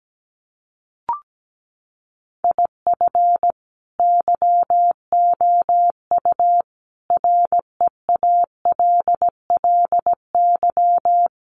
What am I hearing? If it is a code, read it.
Morse code, heard as a keyed tone, about 17 wpm, 700 Hz. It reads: IF YOU REALLY
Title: Sets of 5 Words Encore2 17wpm